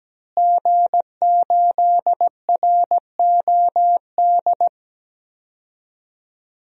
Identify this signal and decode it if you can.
Morse code, heard as a keyed tone, about 17 wpm, 700 Hz. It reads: G8ROD
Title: DXCC-100 Call Signs - 5 Characters - Repeated 3x - 17wpm